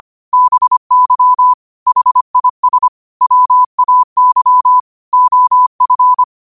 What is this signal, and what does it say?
Morse code, heard as a keyed tone, about 25 wpm, 1000 Hz. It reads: BY HIS WAY OF
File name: qotdfiles / 25wpm